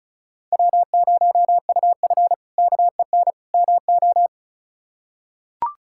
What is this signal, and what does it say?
Morse code, heard as a keyed tone, about 36 wpm, 700 Hz. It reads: W0UF KEN MO
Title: K1USN SST QSO Practice - 35wpm